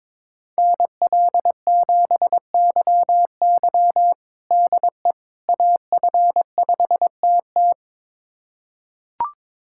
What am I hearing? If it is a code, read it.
Morse code, heard as a keyed tone, about 21 wpm, 700 Hz. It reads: NL7YY DE AF5TT
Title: USA Callsign DE Callsign - 5 Characters - 22wpm